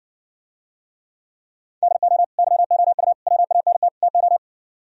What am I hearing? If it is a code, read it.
Morse code, heard as a keyed tone, about 60 wpm, 700 Hz. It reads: DO YOU WANT TO